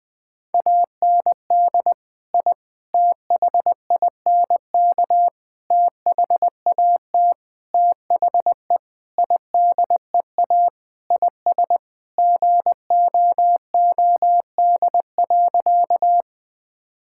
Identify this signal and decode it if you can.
Morse code, heard as a keyed tone, about 21 wpm, 700 Hz. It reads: AND I THINK THAT THE IDEA IS GOOD.
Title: Sentences with 5-Word Phrases - 20wpm